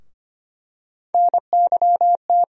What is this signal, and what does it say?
Morse code, heard as a keyed tone, about 25 wpm, 700 Hz. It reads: NYT
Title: USA Call Sign Postfixes - 3 Letters 25wpm